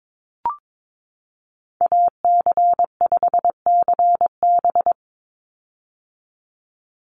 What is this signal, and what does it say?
Morse code, heard as a keyed tone, about 22 wpm, 700 Hz. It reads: AC5CB
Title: USA Call Signs 22wpm